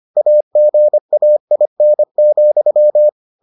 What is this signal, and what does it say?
Morse code, heard as a keyed tone, about 25 wpm, 600 Hz. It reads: AGAIN,